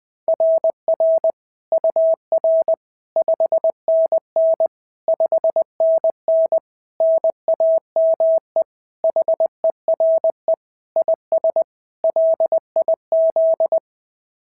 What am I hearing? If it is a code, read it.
Morse code, heard as a keyed tone, about 20 wpm, 690 Hz. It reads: RR UR 5NN 5NN NAME HERE IS LIZ
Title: SOTA QSO Practice 20wpm